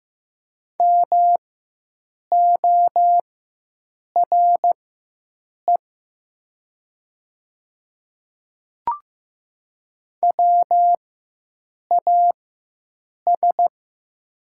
Morse code, heard as a keyed tone, about 15 wpm, spaced as for 7 wpm, 700 Hz. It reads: MORE WAS
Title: 4x Character Spacing - Top 300 Words - 15wpm